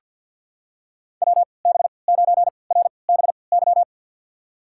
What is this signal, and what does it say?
Morse code, heard as a keyed tone, about 47 wpm, spaced as for 28 wpm, 700 Hz. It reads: WB9RBY
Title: USA Call Signs - 3x Character Spacing - 50wpm